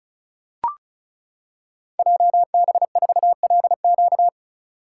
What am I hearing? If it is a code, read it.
Morse code, heard as a keyed tone, about 36 wpm, 700 Hz. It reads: JB4LQ
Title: Code Groups - 5 Letter:Numbers - 35wpm